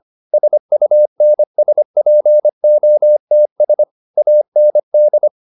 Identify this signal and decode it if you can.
Morse code, heard as a keyed tone, about 25 wpm, 600 Hz. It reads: SUNSPOTS AND
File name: SS311 25